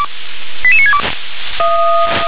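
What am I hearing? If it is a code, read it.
Five-tone selective call (ZVEI), heard as quick runs of five tones, tones of about 70 ms. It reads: 88962
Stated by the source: ZVEI five-tone selective call received on an SDR.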